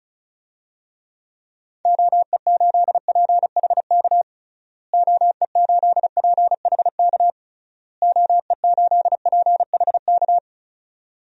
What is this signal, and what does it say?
Morse code, heard as a keyed tone, about 36 wpm, 700 Hz. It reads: OE8PHK OE8PHK OE8PHK
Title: DXCC-100 Call Signs - 6 Characters - Repeated 3x - 35wpm